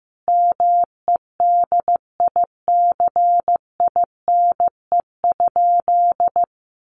Morse code, heard as a keyed tone, about 15 wpm, 700 Hz. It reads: MEDICINE?
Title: Sentences from Top 900 Words 15wpm